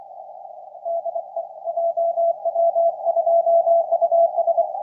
Morse code, heard as a keyed tone, about 28 wpm, 670 Hz. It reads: DE JW2US
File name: jw2us-ru3sd 06-02-2017 2021utc 160m